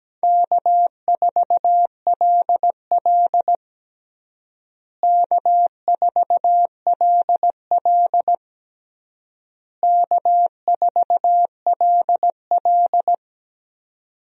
Morse code, heard as a keyed tone, about 17 wpm, 700 Hz. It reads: K4LL K4LL K4LL
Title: USA Call Signs - 4 Characters - Repeated 3x - 17wpm